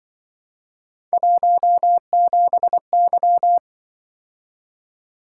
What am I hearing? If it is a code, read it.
Morse code, heard as a keyed tone, about 25 wpm, 700 Hz. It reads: 17Y